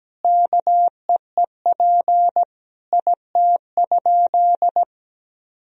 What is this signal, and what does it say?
Morse code, heard as a keyed tone, about 17 wpm, 700 Hz. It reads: KEEP IT?
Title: Sentences from Top 300 Words and Abbreviations 17wpm